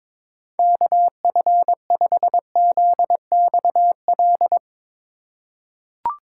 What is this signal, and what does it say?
Morse code, heard as a keyed tone, about 22 wpm, 700 Hz. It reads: KF5ZXL
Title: USA Call Signs - 6 Characters - 2x Character Spacing - 22wpm